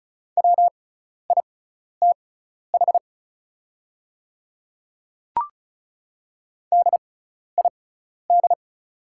Morse code, heard as a keyed tone, about 34 wpm, spaced as for 12 wpm, 700 Hz. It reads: WITH DID
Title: 6x Character Spacing - Top 200 Words - 35wpm